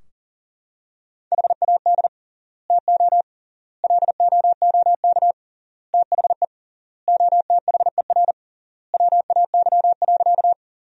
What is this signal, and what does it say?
Morse code, heard as a keyed tone, about 40 wpm, 700 Hz. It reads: HAD TO LOOK THE OTHER WAY.
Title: 3x Word-Spacing - Sentences from Top 100 Words - 40wpm